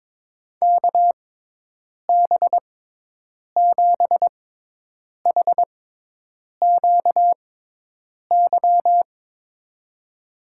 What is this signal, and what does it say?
Morse code, heard as a keyed tone, about 22 wpm, spaced as for 8 wpm, 700 Hz. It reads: KB7HQY
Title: USA Call Signs - 6x Character Spacing - 22wpm